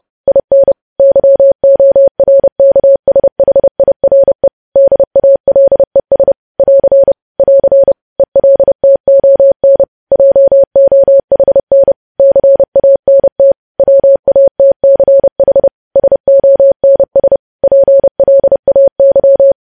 Morse code, heard as a keyed tone, about 30 wpm, 550 Hz. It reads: IN YORKSHIRE DALES <AR> <AR> ELTON JOHN CANT WATCH SONS PLAY
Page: news headlines in morse code at 30 wpm